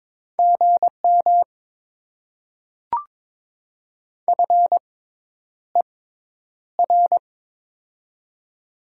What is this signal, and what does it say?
Morse code, heard as a keyed tone, about 22 wpm, spaced as for 10 wpm, 700 Hz. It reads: GM FER
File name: Top 20 QSO Elements - 6x Character Spacing - 22wpm